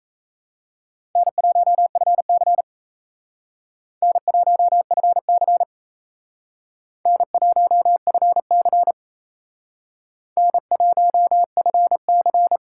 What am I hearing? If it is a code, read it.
Morse code, heard as a keyed tone, about 37 wpm, 700 Hz. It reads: N1FC N1FC N1FC N1FC